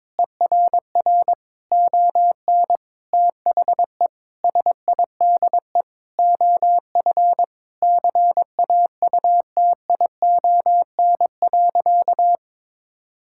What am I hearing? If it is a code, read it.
Morse code, heard as a keyed tone, about 22 wpm, 700 Hz. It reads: ERR ON THE SIDE OF CAUTION.
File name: Sentences with 7-Word Phrases - 22wpm